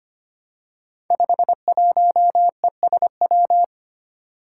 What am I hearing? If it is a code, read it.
Morse code, heard as a keyed tone, about 25 wpm, 700 Hz. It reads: 51ESW
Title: Five Challenging Letter-Number Combinations 25wpm